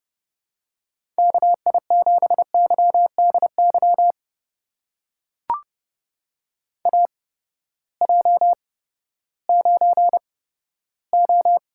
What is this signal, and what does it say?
Morse code, heard as a keyed tone, about 30 wpm, spaced as for 15 wpm, 700 Hz. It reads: KI7YDY AJ9O
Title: USA Call Signs - 8x Character Spacing - 30wpm